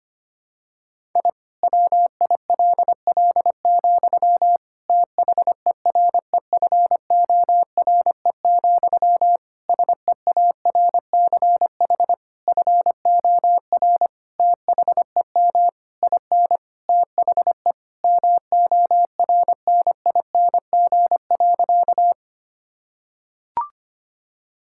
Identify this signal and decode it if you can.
Morse code, heard as a keyed tone, about 25 wpm, 700 Hz. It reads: I WILL, THEREFORE, SEARCH FOR THEM IN THE MORNING.
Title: Sentences from Top 1000 Words - Review Part 5 - 25wpm